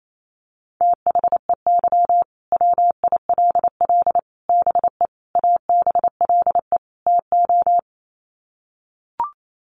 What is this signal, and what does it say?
Morse code, heard as a keyed tone, about 28 wpm, 700 Hz. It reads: THEY WILL BE ABLE TO
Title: Sets of 5 Words Encore 28wpm